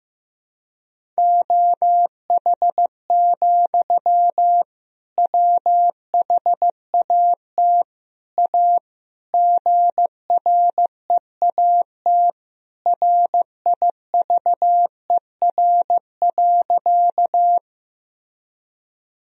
Morse code, heard as a keyed tone, about 15 wpm, 700 Hz. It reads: OH, WHAT A GREAT RIVER.
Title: Sentences from Top 1000 Words - Review Part 1 - 15wpm